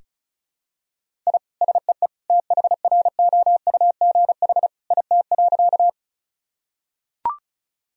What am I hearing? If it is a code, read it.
Morse code, heard as a keyed tone, about 34 wpm, 700 Hz. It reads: I SEE THROUGH IT.
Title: Sentences from Top 1000 Words - Review Part 1 - 35wpm